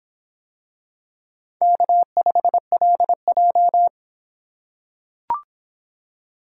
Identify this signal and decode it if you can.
Morse code, heard as a keyed tone, about 25 wpm, 700 Hz. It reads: K5LJ